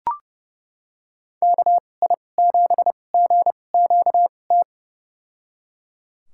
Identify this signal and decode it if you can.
Morse code, heard as a keyed tone, about 30 wpm, spaced as for 22 wpm, 700 Hz. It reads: KI7GQT
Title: USA Call Signs - 2x Character Spacing - 30wpm